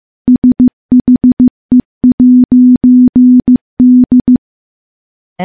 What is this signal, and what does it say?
Morse code, heard as a keyed tone, about 15 wpm, 250 Hz. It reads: SHE'D
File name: morse / morse-training-15WPM